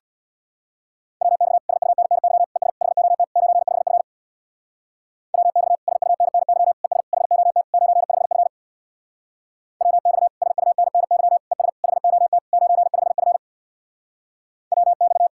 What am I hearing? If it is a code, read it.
Morse code, heard as a keyed tone, about 64 wpm, 700 Hz. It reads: WX SUNNY ES HOT 95F WX SUNNY ES HOT 95F WX SUNNY ES HOT 95F WX